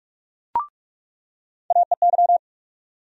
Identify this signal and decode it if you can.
Morse code, heard as a keyed tone, about 42 wpm, 700 Hz. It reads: AEY
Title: USA Call Sign Postfixes - 3 Letters 45wpm